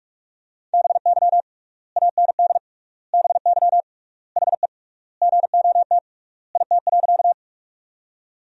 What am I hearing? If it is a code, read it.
Morse code, heard as a keyed tone, about 46 wpm, 700 Hz. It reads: BY AND BY HE GOT IT.
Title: 3x Word-Spacing - Sentences from Top 300 Words - 45wpm